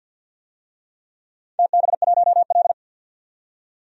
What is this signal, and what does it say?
Morse code, heard as a keyed tone, about 49 wpm, 700 Hz. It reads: TB1L
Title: DXCC-100 Call Signs - 4 Characters - 50wpm